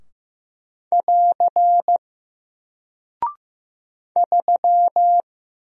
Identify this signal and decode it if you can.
Morse code, heard as a keyed tone, about 15 wpm, 700 Hz. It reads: <AR> 3
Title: Single Character (Letter:Number:Punctuation:Prosign) Alt 15wpm